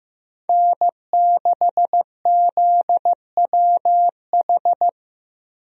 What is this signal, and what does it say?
Morse code, heard as a keyed tone, about 15 wpm, 700 Hz. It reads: N6ZWH
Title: USA Call Signs - 5 Characters - Repeated 2x - 15wpm